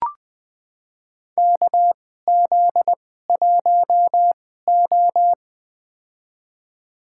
Morse code, heard as a keyed tone, about 20 wpm, spaced as for 14 wpm, 700 Hz. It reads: KZ1O
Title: USA Call Signs - 4 Characters - 2x Character Spacing - 20wpm